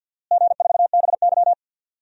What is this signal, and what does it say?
Morse code, heard as a keyed tone, about 48 wpm, 700 Hz. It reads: G4BY
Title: DXCC-100 Call Signs - 4 Characters - Repeated 3x - 50wpm